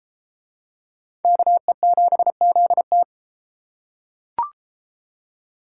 Morse code, heard as a keyed tone, about 30 wpm, 700 Hz. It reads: KE7ZT